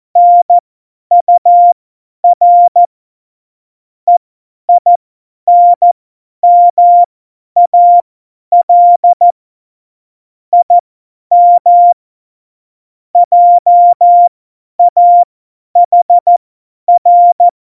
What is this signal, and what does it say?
Morse code, heard as a keyed tone, about 14 wpm, spaced as for 10 wpm, 700 Hz. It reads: NUR EINMAL IM JAHR
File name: Morsekurs / Witze / Witz 28 April 2020, 50 F70